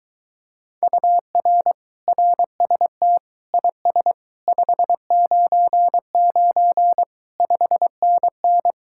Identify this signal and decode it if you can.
Morse code, heard as a keyed tone, about 23 wpm, 700 Hz. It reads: UR RST IS 599 5NN